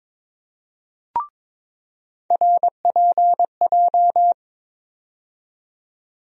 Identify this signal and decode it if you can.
Morse code, heard as a keyed tone, about 22 wpm, 700 Hz. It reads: RPJ
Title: USA Call Sign Postfixes - 3 Letters 22wpm